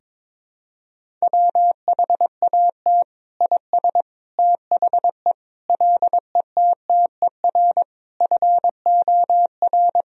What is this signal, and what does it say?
Morse code, heard as a keyed tone, about 22 wpm, 700 Hz. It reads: WHAT IS THE LETTER FOR
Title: Sentences from Top 1000 Words - Review Part 2 - 22wpm